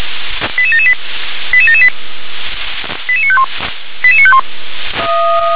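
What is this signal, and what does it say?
Five-tone selective call (ZVEI), heard as quick runs of five tones, tones of about 70 ms. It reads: 88778 88778 88951 88951
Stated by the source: ZVEI five-tone call 88925 received via SDR.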